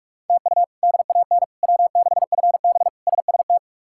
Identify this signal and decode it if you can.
Morse code, heard as a keyed tone, about 44 wpm, 700 Hz. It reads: TU DAN W6FB SST
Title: K1USN SST QSO Practice - 45wpm